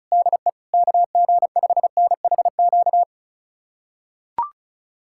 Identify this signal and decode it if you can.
Morse code, heard as a keyed tone, about 36 wpm, 700 Hz. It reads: DE KG5NHQ
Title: USA Callsign DE Callsign 35wpm